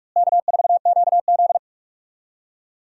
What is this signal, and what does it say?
Morse code, heard as a keyed tone, about 46 wpm, 700 Hz. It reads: K4QZ
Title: USA Call Signs - 4 Characters - Repeated 3x - 45wpm